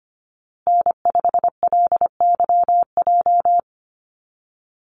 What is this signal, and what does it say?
Morse code, heard as a keyed tone, about 25 wpm, 700 Hz. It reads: N5LYJ
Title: Call Signs Repeated 2x - 25wpm